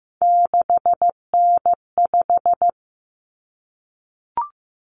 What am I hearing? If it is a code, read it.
Morse code, heard as a keyed tone, about 15 wpm, 700 Hz. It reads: 6N5
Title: Common Cut Numbers - 3-Digit 15wpm